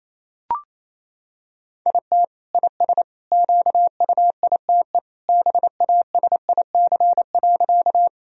Morse code, heard as a keyed tone, about 28 wpm, 700 Hz. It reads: IT IS QUITE BASIC.
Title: Sentences from Top 1000 Words - Review Part 3 - 28wpm